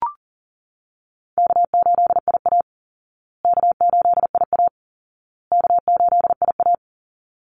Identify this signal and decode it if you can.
Morse code, heard as a keyed tone, about 40 wpm, 700 Hz. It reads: K8IA K8IA K8IA
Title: Easy Call Signs Repeated 3x - 40wpm